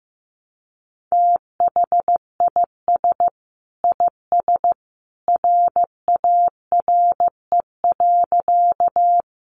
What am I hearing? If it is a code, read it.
Morse code, heard as a keyed tone, about 15 wpm, 700 Hz. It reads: THIS IS RARE.
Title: Sentences from Top 1100 Words 15wpm